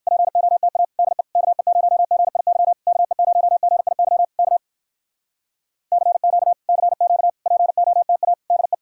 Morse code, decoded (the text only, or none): POTA DE KE0GIY KE0GIY K CQ CQ POTA DE